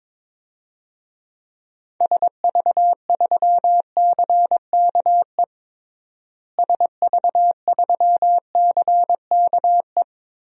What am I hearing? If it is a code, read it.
Morse code, heard as a keyed tone, about 22 wpm, 700 Hz. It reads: SV3CKE SV3CKE
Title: DXCC-100 Call Signs - 6 Characters - Repeated 3x - 22wpm